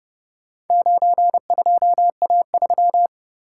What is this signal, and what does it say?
Morse code, heard as a keyed tone, about 29 wpm, 700 Hz. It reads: 92A3